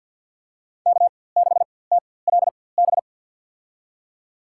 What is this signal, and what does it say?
Morse code, heard as a keyed tone, about 47 wpm, spaced as for 23 wpm, 690 Hz. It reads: K6TLB
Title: USA Call Signs - 5 Characters - 4x Character Spacing - 50wpm